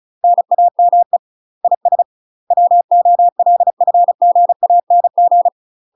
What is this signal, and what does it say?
Morse code, heard as a keyed tone, about 36 wpm, 700 Hz. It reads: NAME IS WOLFGANG
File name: UK MASTER QSOs TRAINING / 35WPM / UK MASTER QSOs 0010